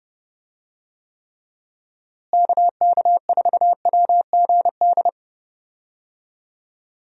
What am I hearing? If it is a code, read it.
Morse code, heard as a keyed tone, about 32 wpm, 700 Hz. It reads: KK4WGD
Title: USA Call Signs - 6 Characters - Repeated 3x - 30wpm